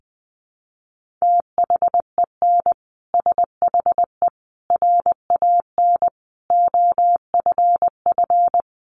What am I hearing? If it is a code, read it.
Morse code, heard as a keyed tone, about 20 wpm, 700 Hz. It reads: THEN SHE RAN OFF
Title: Sentences from Top 1400 Words 20wpm